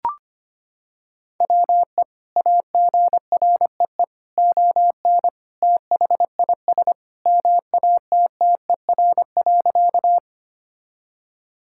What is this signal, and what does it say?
Morse code, heard as a keyed tone, about 25 wpm, 700 Hz. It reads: WE AGREE ON THIS MATTER.
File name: practice / 25wpm / Sentences with 2-Word Phrases - 25wpm